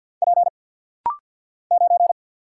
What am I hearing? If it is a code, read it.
Morse code, heard as a keyed tone, about 52 wpm, 700 Hz. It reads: P 9
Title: Single Letter-Number - Rapid-Fire 50wpm